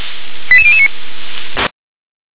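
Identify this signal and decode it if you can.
Five-tone selective call (ZVEI), heard as quick runs of five tones, tones of about 70 ms. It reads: 88009
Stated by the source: ZVEI five-tone call received by SDR.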